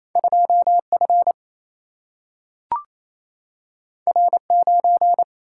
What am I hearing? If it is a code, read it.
Morse code, heard as a keyed tone, about 28 wpm, 700 Hz. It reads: 2 F R 9
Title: Single-Letter Single-Number 28wpm